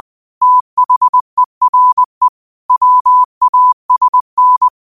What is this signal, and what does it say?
Morse code, heard as a keyed tone, about 20 wpm, 1000 Hz. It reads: THERE WASN
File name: qotdfiles / 20wpm